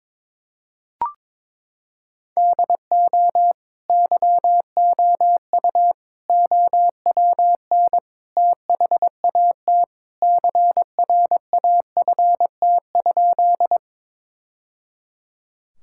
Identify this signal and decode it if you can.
Morse code, heard as a keyed tone, about 22 wpm, 700 Hz. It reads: DO YOU OWN THAT CRAFT?
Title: Sentences from Top 1000 Words - Review Part 6 - 22wpm